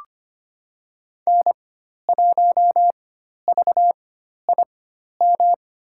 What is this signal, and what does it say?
Morse code, heard as a keyed tone, about 25 wpm, spaced as for 12 wpm, 700 Hz. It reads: N1VIM
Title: USA Call Signs - 5 Characters - 4x Character Spacing - 25wpm